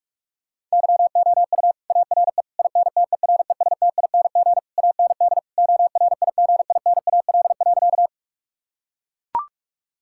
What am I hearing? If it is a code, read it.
Morse code, heard as a keyed tone, about 41 wpm, 700 Hz. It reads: YOU ARE INTERESTING AND ORIGINAL.
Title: Sentences from Top 1000 Words - Review Part 6 - 45wpm